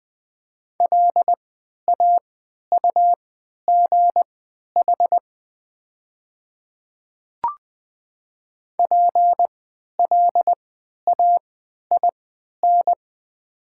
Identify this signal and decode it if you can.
Morse code, heard as a keyed tone, about 20 wpm, spaced as for 11 wpm, 700 Hz. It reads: LAUGH PLAIN
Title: 3x Character Spacing - Top 500 Words - 20wpm